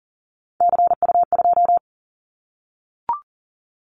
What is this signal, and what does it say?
Morse code, heard as a keyed tone, about 38 wpm, 700 Hz. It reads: CU2
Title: Callsign Prefixes for Top 100 DXCC - 40wpm